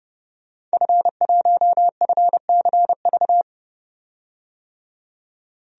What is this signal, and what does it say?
Morse code, heard as a keyed tone, about 30 wpm, 700 Hz. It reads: F1FCV
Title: DXCC-100 Call Signs - 5 Characters - 30wpm